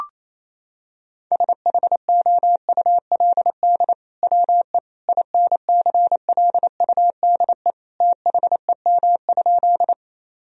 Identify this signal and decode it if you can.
Morse code, heard as a keyed tone, about 28 wpm, 700 Hz. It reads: SHOULD WE INCLUDE THEM?
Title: Sentences from Top 1000 Words - Review Part 3 - 28wpm